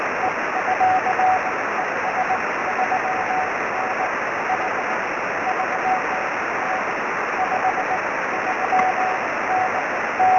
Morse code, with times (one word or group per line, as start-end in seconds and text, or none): none